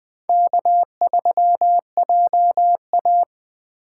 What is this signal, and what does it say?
Morse code, heard as a keyed tone, about 20 wpm, 700 Hz. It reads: K3JA
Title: 2023 POTA Activator Call Signs - 20wpm